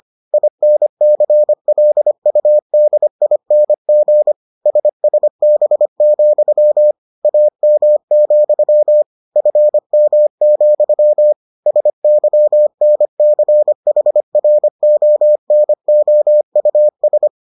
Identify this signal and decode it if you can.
Morse code, heard as a keyed tone, about 25 wpm, 600 Hz. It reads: INCLUDING SSB, AM, FM, SYNCHRONOUS